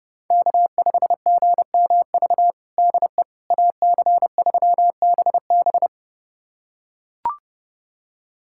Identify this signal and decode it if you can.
Morse code, heard as a keyed tone, about 30 wpm, 700 Hz. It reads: K5GMV DE AC3BB
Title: USA Callsign DE Callsign - 5 Characters - 30wpm